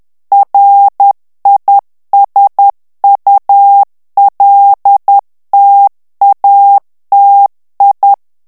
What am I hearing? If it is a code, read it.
Morse code, heard as a keyed tone, about 10 wpm, 800 Hz. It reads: RISULTATI
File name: ITALIEN texte clair 2 vitesse 10 mots minute